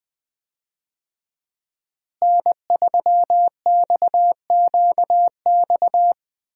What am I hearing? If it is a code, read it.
Morse code, heard as a keyed tone, about 20 wpm, 700 Hz. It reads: N3XQX
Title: USA Call Signs - 5 Characters - Repeated 2x - 20wpm